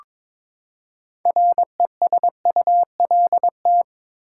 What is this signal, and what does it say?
Morse code, heard as a keyed tone, about 22 wpm, 700 Hz. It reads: RESULT
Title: Six Letter Words - 22wpm